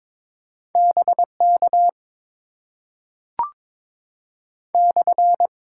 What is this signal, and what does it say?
Morse code, heard as a keyed tone, about 22 wpm, 700 Hz. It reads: BK /
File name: Top 90 QSO Elements - 8x Character Spacing - 22wpm